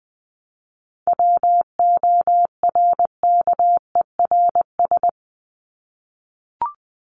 Morse code, heard as a keyed tone, about 20 wpm, 700 Hz. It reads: WORKERS
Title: ES:S Suffix - Top 100 Words with Base Words 20wpm